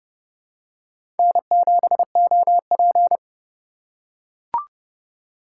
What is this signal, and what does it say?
Morse code, heard as a keyed tone, about 30 wpm, 700 Hz. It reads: N7OP
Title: USA Call Signs - 4 Characters - 4x Character Spacing - 30wpm